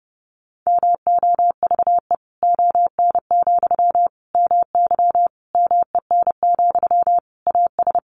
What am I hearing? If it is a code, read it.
Morse code, heard as a keyed tone, about 30 wpm, 700 Hz. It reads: MOVE ON, MY MEN, AS
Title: Sentences from Top 500 Words 30wpm